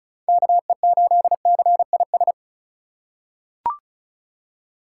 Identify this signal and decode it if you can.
Morse code, heard as a keyed tone, about 34 wpm, 700 Hz. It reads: KE8CIS
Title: USA Call Signs - 6 Characters - 3x Character Spacing - 35wpm